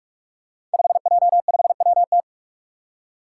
Morse code, heard as a keyed tone, about 46 wpm, 700 Hz. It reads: 5J5WT
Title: Five Challenging Letter-Number Combinations 45wpm